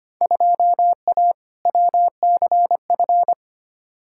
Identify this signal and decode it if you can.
Morse code, heard as a keyed tone, about 25 wpm, 700 Hz. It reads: 2A WCF
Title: Field Day Exchanges Encore - 25wpm